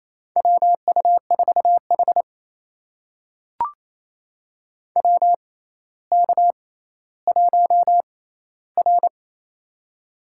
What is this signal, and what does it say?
Morse code, heard as a keyed tone, about 27 wpm, spaced as for 14 wpm, 700 Hz. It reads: WU4H WK1R
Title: USA Call Signs - 4 Characters - 6x Character Spacing - 28wpm